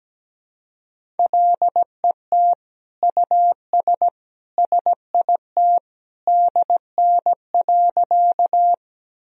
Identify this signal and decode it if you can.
Morse code, heard as a keyed tone, about 17 wpm, 700 Hz. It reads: LET US SIT DN.
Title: Sentences from Top 500 Words and Abbreviations 17wpm